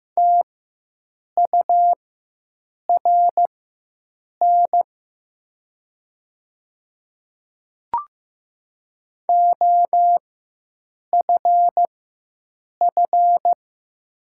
Morse code, heard as a keyed tone, about 15 wpm, spaced as for 7 wpm, 700 Hz. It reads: TURN OFF
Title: 4x Character Spacing - Top 300 Words - 15wpm